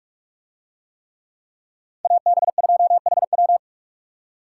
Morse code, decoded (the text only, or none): AB2HW